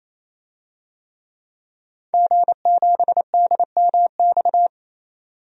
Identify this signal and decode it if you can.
Morse code, heard as a keyed tone, about 28 wpm, 700 Hz. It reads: G7DMX
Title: DXCC-100 Call Signs - 5 Characters - 28wpm